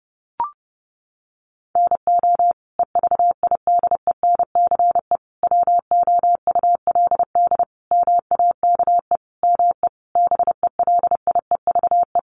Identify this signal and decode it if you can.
Morse code, heard as a keyed tone, about 30 wpm, 700 Hz. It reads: NO EVIDENCE WOULD MAKE ME BELIEVE
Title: Sentences from Top 1300 Words 30wpm